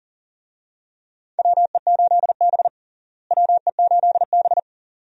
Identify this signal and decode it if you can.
Morse code, heard as a keyed tone, about 37 wpm, 700 Hz. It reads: WE8B WE8B
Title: USA Call Signs - 4 Characters - Repeated 2x - 40wpm